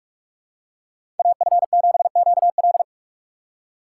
Morse code, heard as a keyed tone, about 42 wpm, 700 Hz. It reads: AF7QL
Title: USA Call Signs - 4x Character Spacing - 45wpm